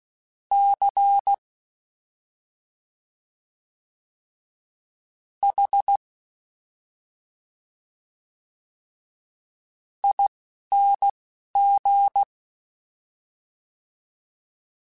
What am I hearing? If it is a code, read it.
Morse code, heard as a keyed tone, about 16 wpm, spaced as for 4 wpm, 800 Hz. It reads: CHING
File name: Special INGs-mp3s / Special INGs-mp3s / Slow ing @10